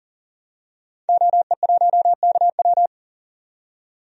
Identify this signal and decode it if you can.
Morse code, heard as a keyed tone, about 38 wpm, 700 Hz. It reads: OE1KW
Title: DXCC-100 Call Signs - 5 Characters - 40wpm